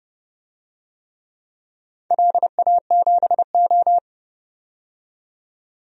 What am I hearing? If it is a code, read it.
Morse code, heard as a keyed tone, about 30 wpm, 700 Hz. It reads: LA7O
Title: DXCC-100 Call Signs - 4 Characters - 30wpm